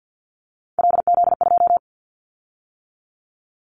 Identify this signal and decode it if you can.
Morse code, heard as a keyed tone, about 50 wpm, 700 Hz. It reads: LZ2
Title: Callsign Prefixes for Top 100 DXCC - 50wpm